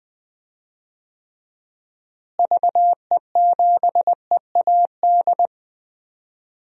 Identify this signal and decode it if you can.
Morse code, heard as a keyed tone, about 20 wpm, 700 Hz. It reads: VE7EAD
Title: DXCC-100 Call Signs - 6 Characters - Repeated 3x - 20wpm